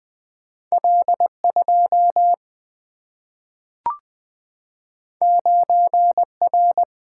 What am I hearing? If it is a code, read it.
Morse code, heard as a keyed tone, about 20 wpm, 700 Hz. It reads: L2 9R
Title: Single-Letter Single-Number 20wpm